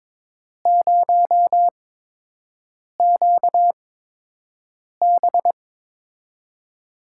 Morse code, heard as a keyed tone, about 22 wpm, spaced as for 6 wpm, 700 Hz. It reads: 0QB
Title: USA Call Signs - 8x Character Spacing - 22wpm